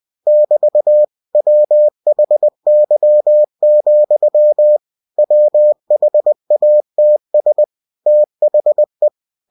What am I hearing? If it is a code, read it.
Morse code, heard as a keyed tone, about 20 wpm, 600 Hz. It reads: <BT> WHY, WHATS THE
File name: cw-books / winnie-the-pooh / winnie-20wpm-page-0065